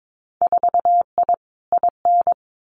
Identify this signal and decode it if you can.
Morse code, heard as a keyed tone, about 22 wpm, 700 Hz. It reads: 4I IN
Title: Winter Field Day Exchanges 22wpm